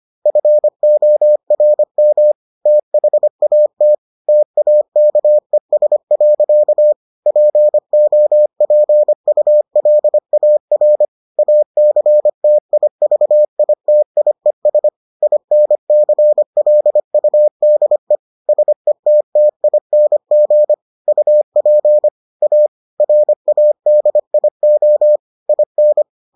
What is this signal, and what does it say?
Morse code, heard as a keyed tone, about 25 wpm, 600 Hz. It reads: FORM THAT TAKES. POPULAR ACTIVITIES INCLUDE SETTING UP A RADIO IN